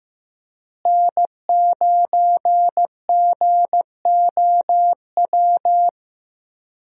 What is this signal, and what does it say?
Morse code, heard as a keyed tone, about 15 wpm, 700 Hz. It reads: N9GOW
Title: USA Call Signs - 5 Characters - Repeated 2x - 15wpm